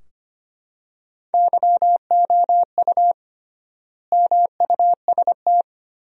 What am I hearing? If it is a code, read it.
Morse code, heard as a keyed tone, about 25 wpm, 700 Hz. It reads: YOU MUST
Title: 3x Word-Spacing - Sentences from Top 400 Words - 25wpm